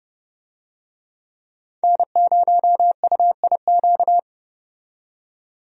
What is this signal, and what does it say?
Morse code, heard as a keyed tone, about 30 wpm, 700 Hz. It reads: N0UIQ
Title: USA Call Signs - 5 Characters - 30wpm